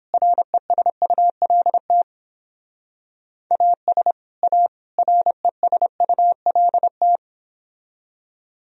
Morse code, heard as a keyed tone, about 29 wpm, 700 Hz. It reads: RESULT AS A RESULT